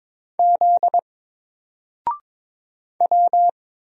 Morse code, heard as a keyed tone, about 22 wpm, 700 Hz. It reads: Z W
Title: Single Letter-Number Alt 22wpm